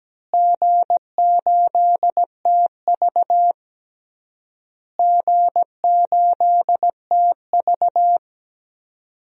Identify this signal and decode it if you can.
Morse code, heard as a keyed tone, about 17 wpm, 700 Hz. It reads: G8TV G8TV
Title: DXCC-100 Call Signs - 4 Characters - Repeated 3x - 17wpm